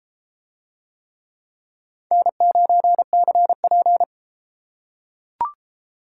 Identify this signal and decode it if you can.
Morse code, heard as a keyed tone, about 33 wpm, 700 Hz. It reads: N9CP